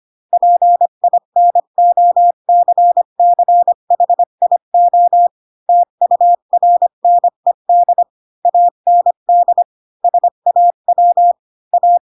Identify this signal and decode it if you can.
Morse code, heard as a keyed tone, about 25 wpm, 700 Hz. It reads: PINOCCHIO TURNED AND SAW A
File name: Pinocchio / Pinocchio ch4 20 25 ik1hge